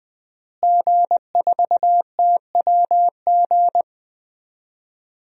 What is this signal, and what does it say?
Morse code, heard as a keyed tone, about 20 wpm, 700 Hz. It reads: G4TWG
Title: DXCC-100 Call Signs - 5 Characters - Repeated 2x - 20wpm